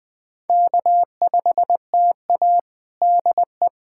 Morse code, heard as a keyed tone, about 20 wpm, 700 Hz. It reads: K5TA DE
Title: Practice QSO - Exchange 4 - Ending - 20wpm